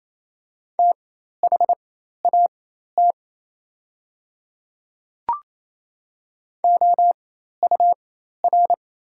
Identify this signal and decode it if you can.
Morse code, heard as a keyed tone, about 28 wpm, spaced as for 13 wpm, 700 Hz. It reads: THAT OUR
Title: 4x Character Spacing - Top 200 Words - 28wpm